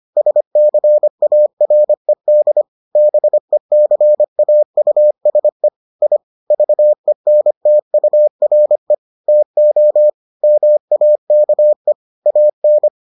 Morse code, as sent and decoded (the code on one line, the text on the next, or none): ... -.-. .- .-. -.. / -... -.-. .- ..- ... / .. / ...- . -. - ..- .-. / - --- / -- .- -.- . / .- -.
SCARED BECAUSE I VENTURE TO MAKE AN